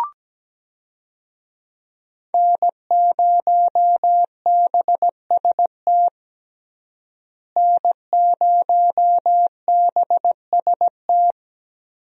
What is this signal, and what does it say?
Morse code, heard as a keyed tone, about 17 wpm, 700 Hz. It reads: N0BST N0BST
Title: USA Call Signs - 5 Characters - Repeated 2x - 17wpm